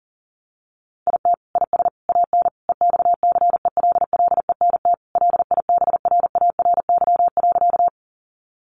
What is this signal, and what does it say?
Morse code, heard as a keyed tone, about 40 wpm, 700 Hz. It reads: IT IS AN EXCELLENT LIBRARY.
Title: Sentences from Top 1000 Words 40wpm